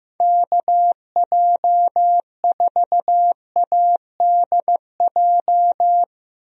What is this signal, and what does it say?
Morse code, heard as a keyed tone, about 15 wpm, 700 Hz. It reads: KJ4ADJ
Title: USA Call Signs - 6 Characters - 15wpm